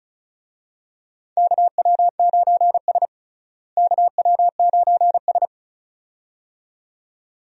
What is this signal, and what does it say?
Morse code, heard as a keyed tone, about 34 wpm, 700 Hz. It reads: KW9S KW9S
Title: USA Call Signs - 4 Characters - Repeated 2x - 35wpm